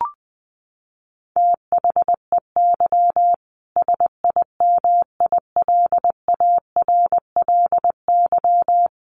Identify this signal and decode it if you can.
Morse code, heard as a keyed tone, about 20 wpm, 700 Hz. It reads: THEY SIMILARLY
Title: Sentences from Top 1400 Words 20wpm